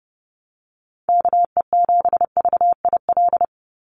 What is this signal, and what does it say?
Morse code, heard as a keyed tone, about 30 wpm, 700 Hz. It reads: KE7VIL
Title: USA Call Signs 30wpm